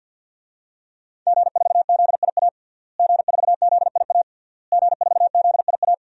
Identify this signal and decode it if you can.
Morse code, heard as a keyed tone, about 52 wpm, 700 Hz. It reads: G4ZIA G4ZIA G4ZIA
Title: DXCC-100 Call Signs - 5 Characters - Repeated 3x - 50wpm